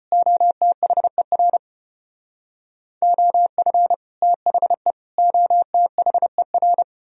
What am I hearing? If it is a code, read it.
Morse code, heard as a keyed tone, about 34 wpm, 700 Hz. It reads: OTHER OF THE OTHER